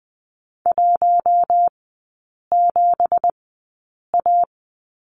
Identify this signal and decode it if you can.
Morse code, heard as a keyed tone, about 20 wpm, spaced as for 8 wpm, 700 Hz. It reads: 17A
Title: 178 Lesson 29e - Slash - Three Characters 20wpm